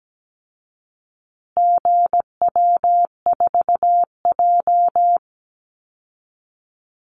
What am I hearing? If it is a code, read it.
Morse code, heard as a keyed tone, about 17 wpm, 700 Hz. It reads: GW4J
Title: Call Signs Repeated 2x - 17wpm